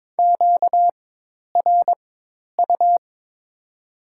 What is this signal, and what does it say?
Morse code, heard as a keyed tone, about 22 wpm, spaced as for 10 wpm, 700 Hz. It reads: QRU
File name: Top 90 QSO Elements - 4x Character Spacing - 22wpm